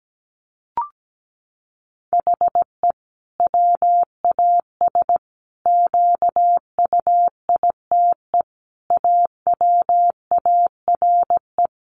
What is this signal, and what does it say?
Morse code, heard as a keyed tone, about 17 wpm, 700 Hz. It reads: HE WAS QUITE AWARE
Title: Sentences from Top 1400 Words 17wpm